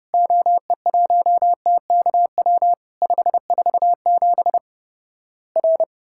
Morse code, heard as a keyed tone, about 30 wpm, 700 Hz. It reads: OE1TKW 547 R
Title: Worked All Europe DX Contest - QTC Exchange - 30wpm